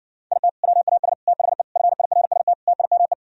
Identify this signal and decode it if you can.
Morse code, heard as a keyed tone, about 60 wpm, 700 Hz. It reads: IT WAS THE FIRST TIME